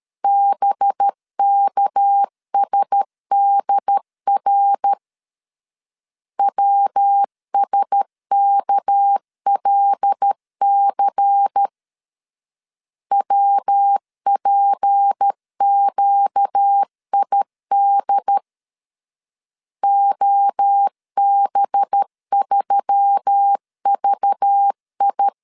Morse code, heard as a keyed tone, about 13 wpm, 800 Hz. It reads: BKSDR WSKLC WPQID OB3VI
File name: LEZIONI / 6 PKX0 / 6 PK 12wpm 2006